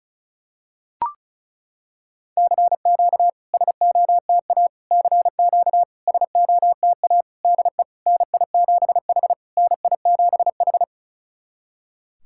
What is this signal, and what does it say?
Morse code, heard as a keyed tone, about 36 wpm, 700 Hz. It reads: CQ SOTA CQ SOTA DE NI7H NI7H
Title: SOTA QSO Practice 35wpm